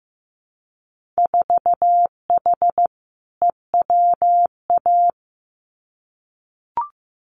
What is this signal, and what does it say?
Morse code, heard as a keyed tone, about 15 wpm, 700 Hz. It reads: 4H EWA
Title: Winter Field Day Exchanges Encore 15wpm